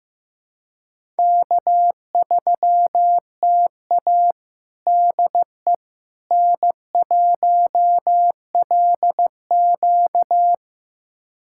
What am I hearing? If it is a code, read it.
Morse code, heard as a keyed tone, about 15 wpm, 700 Hz. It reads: K3TA DE N1LQ
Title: USA Callsign DE Callsign - 4 Characters - 15wpm